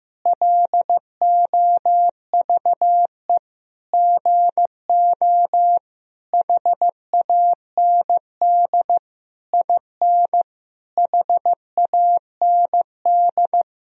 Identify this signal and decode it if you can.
Morse code, heard as a keyed tone, about 15 wpm, 700 Hz. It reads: LOVE GO HAND IN HAND
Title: Sentences from Top 1000 Words - Review Part 3 - 15wpm